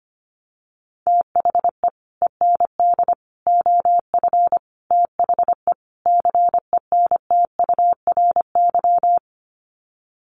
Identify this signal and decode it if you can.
Morse code, heard as a keyed tone, about 25 wpm, 700 Hz. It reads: THE END OF THE CENTURY
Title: Sets of 5 Words Encore2 25wpm